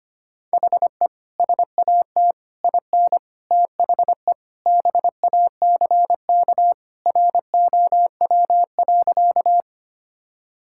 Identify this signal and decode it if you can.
Morse code, heard as a keyed tone, about 25 wpm, 700 Hz. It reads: HE SAT IN THE BACK ROW.
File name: practice / 25wpm / Sentences with 3-Word Phrases - 25wpm